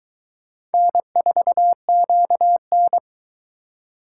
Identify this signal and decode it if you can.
Morse code, heard as a keyed tone, about 21 wpm, 700 Hz. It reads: N4QN